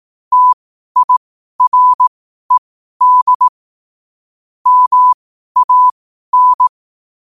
Morse code, heard as a keyed tone, about 18 wpm, spaced as for 12 wpm, 1000 Hz. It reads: TIRED MAN
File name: qotdfiles / 12wpm